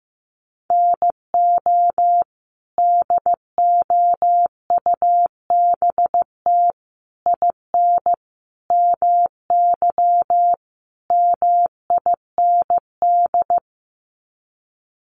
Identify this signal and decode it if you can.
Morse code, heard as a keyed tone, about 15 wpm, 700 Hz. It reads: NO DOUBT IN MY MIND
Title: Sets of 5 Words Encore2 15wpm